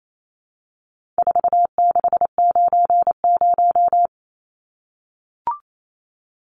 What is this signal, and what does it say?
Morse code, heard as a keyed tone, about 28 wpm, 700 Hz. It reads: 4690
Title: Numbers - 4-Digit 28wpm